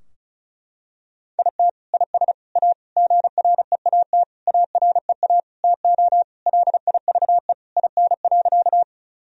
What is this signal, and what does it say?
Morse code, heard as a keyed tone, about 34 wpm, 700 Hz. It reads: IT IS A GREAT AREA TO LIVE IN.
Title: Sentences from Top 1000 Words - Review Part 2 - 35wpm